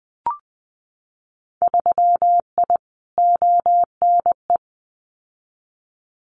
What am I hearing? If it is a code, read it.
Morse code, heard as a keyed tone, about 19 wpm, 700 Hz. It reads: 3I ONE
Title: Winter Field Day Exchanges Encore 20wpm